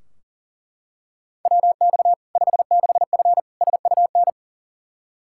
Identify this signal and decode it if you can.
Morse code, heard as a keyed tone, about 40 wpm, 700 Hz. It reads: WX 56F SUN
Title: Practice QSO - Exchange 2 - Info - 40wpm